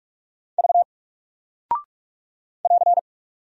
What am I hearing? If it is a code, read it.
Morse code, heard as a keyed tone, about 44 wpm, 700 Hz. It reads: V <AR>
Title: Single Character (Letter:Number:Punctuation:Prosign) 45wpm